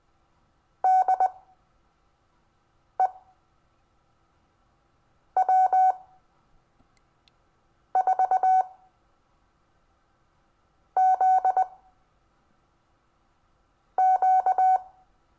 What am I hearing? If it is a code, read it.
Morse code, heard as a keyed tone, about 20 wpm, spaced as for 4 wpm, 750 Hz. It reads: DE W4ZQ
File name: cw / Lesson22 REVIEW